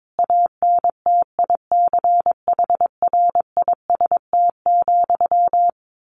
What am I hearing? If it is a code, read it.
Morse code, heard as a keyed tone, about 22 wpm, 700 Hz. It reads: ANTICHRIST,
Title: A Sound Of Thunder 22wpm